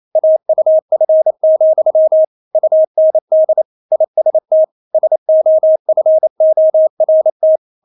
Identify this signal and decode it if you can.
Morse code, heard as a keyed tone, about 28 wpm, 630 Hz. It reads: AUF, UND IST SOFORT
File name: Witz9 28wpm